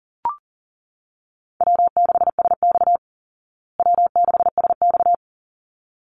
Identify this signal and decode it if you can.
Morse code, heard as a keyed tone, about 40 wpm, 700 Hz. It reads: W6SX W6SX
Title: Easy Call Signs Repeated 2x - 40wpm